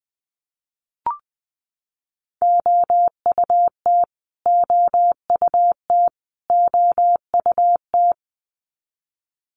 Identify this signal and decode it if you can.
Morse code, heard as a keyed tone, about 20 wpm, 700 Hz. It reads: OUT OUT OUT
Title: Top 300 Words Repeated 20wpm